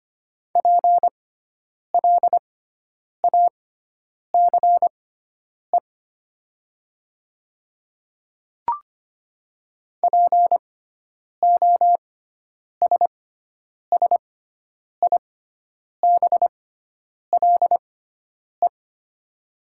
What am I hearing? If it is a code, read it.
Morse code, heard as a keyed tone, about 25 wpm, spaced as for 9 wpm, 700 Hz. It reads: PLACE POSSIBLE
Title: 6x Character Spacing - Top 500 Words - 25wpm